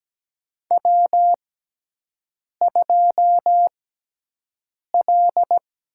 Morse code, heard as a keyed tone, about 17 wpm, spaced as for 6 wpm, 700 Hz. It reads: W2L
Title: USA Call Signs - 6x Character Spacing - 17wpm